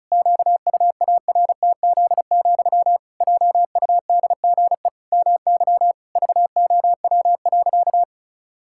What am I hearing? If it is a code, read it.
Morse code, heard as a keyed tone, about 36 wpm, 700 Hz. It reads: QUARTZ, JUDGE MY VOW.
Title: Pangram Challenge v2 - 35wpm